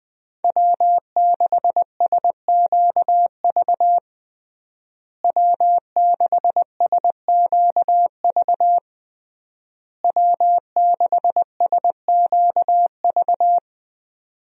Morse code, heard as a keyed tone, about 20 wpm, 700 Hz. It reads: W6SQV W6SQV W6SQV
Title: USA Call Signs - 5 Characters - Repeated 3x - 20wpm